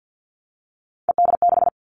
Morse code, heard as a keyed tone, about 50 wpm, 700 Hz. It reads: ED6
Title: Three Challenging Letter-Number Combinations 50wpm